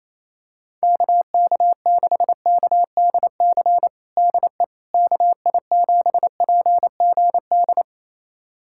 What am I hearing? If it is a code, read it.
Morse code, heard as a keyed tone, about 28 wpm, 700 Hz. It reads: KK6KDC DE KI7PGD
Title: USA Callsign DE Callsign 28wpm